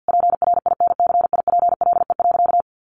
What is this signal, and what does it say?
Morse code, heard as a keyed tone, about 50 wpm, 700 Hz. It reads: PRINCIPLE.
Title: Sentences from Top 1400 Words 50wpm